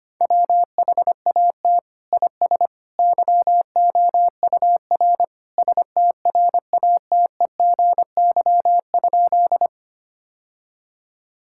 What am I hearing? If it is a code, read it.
Morse code, heard as a keyed tone, about 25 wpm, 700 Hz. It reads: WHAT IS YOUR STRATEGY?
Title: Sentences from Top 1000 Words - Review Part 5 - 25wpm